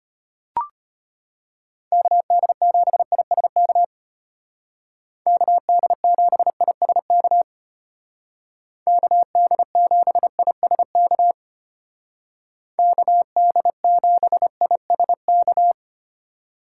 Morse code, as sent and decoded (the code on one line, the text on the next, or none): -.- -.. --... .. ... -.- / -.- -.. --... .. ... -.- / -.- -.. --... .. ... -.- / -.- -.. --... .. ... -.-
KD7ISK KD7ISK KD7ISK KD7ISK